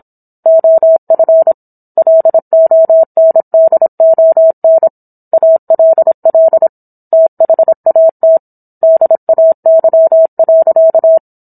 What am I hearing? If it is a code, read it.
Morse code, heard as a keyed tone, about 27 wpm, 650 Hz. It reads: OF LONDON ALL THAT DAY.
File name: Book One / 17-Chapter 17 - 26 WPM